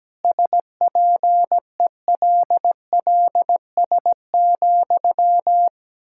Morse code, heard as a keyed tone, about 17 wpm, 700 Hz. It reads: SPELLS,
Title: Pangram Challenge v2 - 17wpm